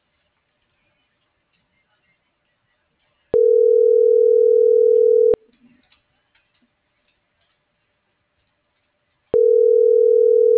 Ringback tone (North America), 440 + 480 Hz.
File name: CENTRAL SOTOVA / out alejandro.rodriguez